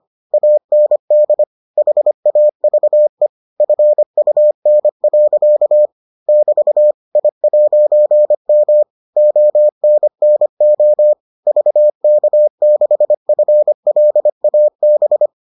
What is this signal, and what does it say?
Morse code, heard as a keyed tone, about 25 wpm, 600 Hz. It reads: AND HAVE FUN. <BT> I'M ONNO VK6FLAB